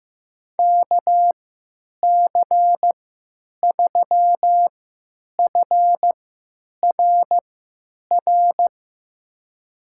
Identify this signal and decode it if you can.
Morse code, heard as a keyed tone, about 15 wpm, spaced as for 8 wpm, 700 Hz. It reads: KC3FRR
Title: USA Call Signs - 3x Character Spacing - 15wpm